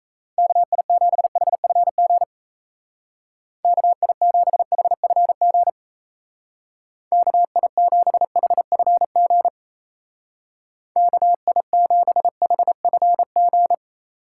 Morse code, heard as a keyed tone, about 33 wpm, 700 Hz. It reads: KI7HFG KI7HFG KI7HFG KI7HFG
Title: Speed Racing - USA Call Signs 28wpm